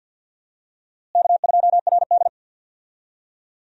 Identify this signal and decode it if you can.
Morse code, heard as a keyed tone, about 52 wpm, 700 Hz. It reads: K2RD
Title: USA Call Signs - 4 Characters - Repeated 3x - 50wpm